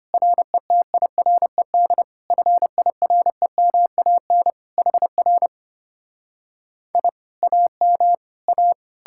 Morse code, heard as a keyed tone, about 33 wpm, 700 Hz. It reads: RETIRED FIREMAN HR I AM A